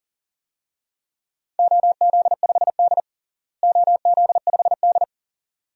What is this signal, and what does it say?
Morse code, heard as a keyed tone, about 40 wpm, 700 Hz. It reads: OZ5D OZ5D
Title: DXCC-100 Call Signs - 4 Characters - Repeated 2x - 40wpm